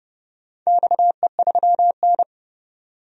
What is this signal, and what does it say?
Morse code, heard as a keyed tone, about 30 wpm, 700 Hz. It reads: XE3N
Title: DXCC-100 Call Signs - 4 Characters - 30wpm